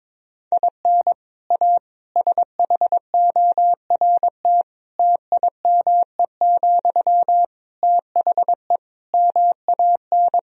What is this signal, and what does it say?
Morse code, heard as a keyed tone, about 22 wpm, 700 Hz. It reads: IN A SHORT TIME, THE MAN
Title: Sentences from Top 1000 Words - Review Part 2 - 22wpm